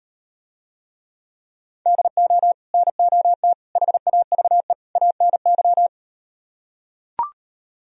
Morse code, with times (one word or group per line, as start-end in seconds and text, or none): 1.9-2.5 DO
2.7-3.5 NOT
3.8-4.7 HAVE
5.0-5.9 ANY